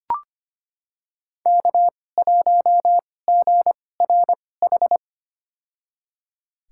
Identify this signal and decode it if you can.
Morse code, heard as a keyed tone, about 25 wpm, spaced as for 18 wpm, 700 Hz. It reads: K1GRH
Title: USA Call Signs - 5 Characters - 2x Character Spacing - 25wpm